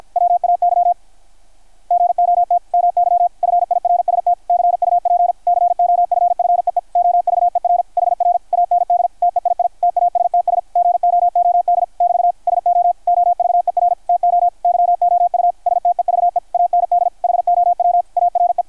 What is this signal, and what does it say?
Morse code, heard as a keyed tone, about 53 wpm, 660 Hz, with radio noise. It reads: WAY GOT MY FIRST CUP COFFEE OVER HR AND TEII TASTS GOOD <BT> SO OVER TO YOU STEVE AND HOW ARE
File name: archived / CRSnet